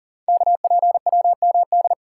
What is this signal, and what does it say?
Morse code, heard as a keyed tone, about 38 wpm, 690 Hz. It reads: KPWMD
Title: Code Groups - 5 Letters - 40wpm